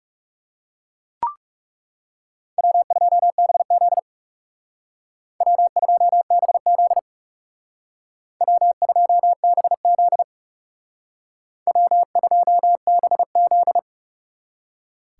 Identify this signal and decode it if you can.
Morse code, heard as a keyed tone, about 38 wpm, 700 Hz. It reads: W2BZ W2BZ W2BZ W2BZ